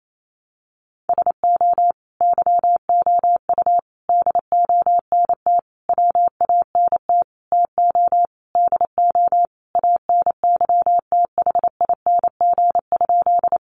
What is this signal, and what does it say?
Morse code, heard as a keyed tone, about 28 wpm, 700 Hz. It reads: SO YOU DONT WANT TO DO ANYTHING?
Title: Sentences from Top 1000 Words 28wpm